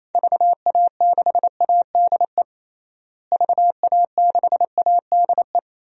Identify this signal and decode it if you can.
Morse code, heard as a keyed tone, about 28 wpm, 700 Hz. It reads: VA6ADE VA6ADE
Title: DXCC-100 Call Signs - 6 Characters - Repeated 2x - 28wpm